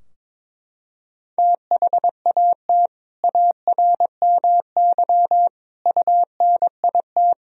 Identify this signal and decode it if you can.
Morse code, heard as a keyed tone, about 22 wpm, 700 Hz. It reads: THAT ARMY UNIT
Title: Sentences from Top 1000 Words - Review Part 2 - 22wpm